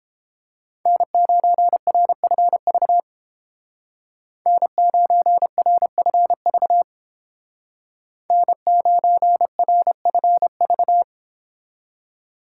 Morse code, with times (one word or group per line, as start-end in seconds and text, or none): 0.9-3.0 N9RFV
4.5-6.8 N9RFV
8.3-11.0 N9RFV